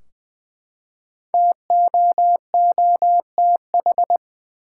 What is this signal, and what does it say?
Morse code, heard as a keyed tone, about 21 wpm, 700 Hz. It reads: TOOTH
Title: Five Letter Words - 20wpm